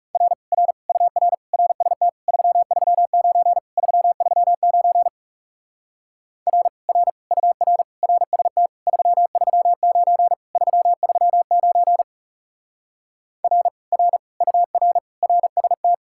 Morse code, heard as a keyed tone, about 40 wpm, 700 Hz. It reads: R R UR RST 339 339 R R UR RST 339 339 R R UR RST